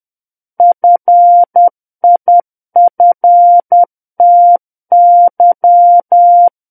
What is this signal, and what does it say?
Morse code, heard as a keyed tone, about 10 wpm, 700 Hz. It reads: FIFTY
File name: Book One / 01-Chapter 1 - 10 WPM